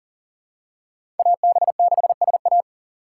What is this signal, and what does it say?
Morse code, heard as a keyed tone, about 38 wpm, 700 Hz. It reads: AB6SA
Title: USA Call Signs - 5 Characters - 4x Character Spacing - 40wpm